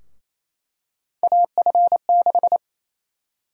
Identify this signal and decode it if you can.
Morse code, heard as a keyed tone, about 28 wpm, 700 Hz. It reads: AF6
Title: USA Call Sign Prefixes - Repeated 2x - 28wpm